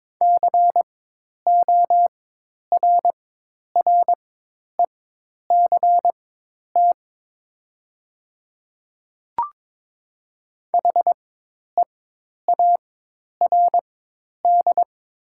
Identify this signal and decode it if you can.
Morse code, heard as a keyed tone, about 21 wpm, spaced as for 10 wpm, 700 Hz. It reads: CORRECT HEARD
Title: 4x Character Spacing - Top 500 Words - 22wpm